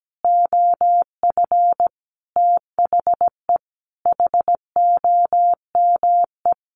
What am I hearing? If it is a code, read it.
Morse code, heard as a keyed tone, about 17 wpm, 700 Hz. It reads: OF THE HOME
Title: Sentences from Top 1400 Words 17wpm